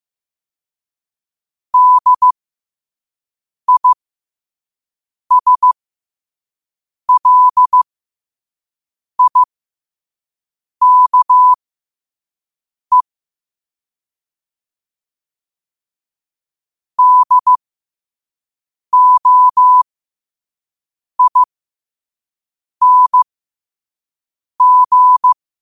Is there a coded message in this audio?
Yes: Morse code from 1.7 to 25.3 s, heard as a keyed tone, about 15 wpm, spaced as for 5 wpm, 1000 Hz. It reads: DISLIKE DOING